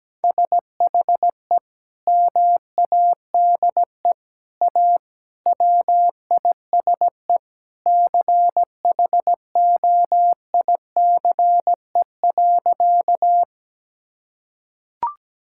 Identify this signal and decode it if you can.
Morse code, heard as a keyed tone, about 17 wpm, 700 Hz. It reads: SHE MADE A WISE CHOICE.
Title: Sentences from Top 1600 Words 17wpm